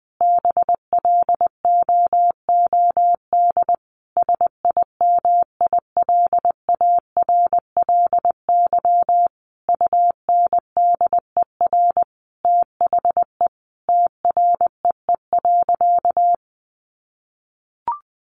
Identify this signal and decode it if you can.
Morse code, heard as a keyed tone, about 19 wpm, 700 Hz. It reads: BLOOD SIMILARLY UNDER THE TREE.
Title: Sentences from Top 1400 Words 20wpm